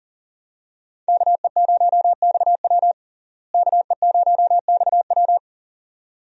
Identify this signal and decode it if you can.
Morse code, heard as a keyed tone, about 40 wpm, 700 Hz. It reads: KE0XW KE0XW
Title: USA Call Signs - 5 Characters - Repeated 2x - 40wpm